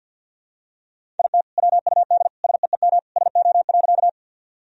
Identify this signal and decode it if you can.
Morse code, heard as a keyed tone, about 46 wpm, 700 Hz. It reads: IT WUD SEEM SO.
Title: Sentences from Top 300 Words and Abbreviations 50wpm